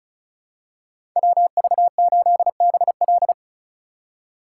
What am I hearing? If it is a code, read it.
Morse code, heard as a keyed tone, about 34 wpm, 700 Hz. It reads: WV8BL
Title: USA Call Signs - 5 Characters - 3x Character Spacing - 35wpm